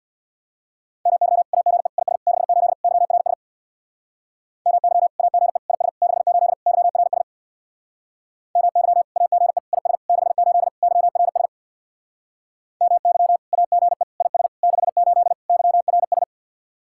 Morse code, heard as a keyed tone, about 64 wpm, 700 Hz. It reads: MY AGE IS 68 YRS MY AGE IS 68 YRS MY AGE IS 68 YRS MY AGE IS 68 YRS